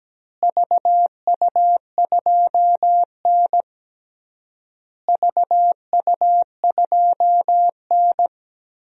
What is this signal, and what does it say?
Morse code, heard as a keyed tone, about 17 wpm, 700 Hz. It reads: VU2N VU2N
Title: DXCC-100 Call Signs - 4 Characters - Repeated 3x - 17wpm